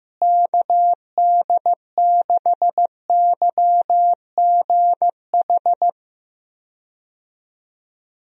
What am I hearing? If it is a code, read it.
Morse code, heard as a keyed tone, about 15 wpm, 700 Hz. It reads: KD6YGH
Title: USA Call Signs - 6 Characters - Repeated 2x - 15wpm